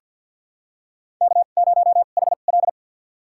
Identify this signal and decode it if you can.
Morse code, heard as a keyed tone, about 46 wpm, spaced as for 34 wpm, 700 Hz. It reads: K0HL
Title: USA Call Signs - 4 Characters - 2x Character Spacing - 50wpm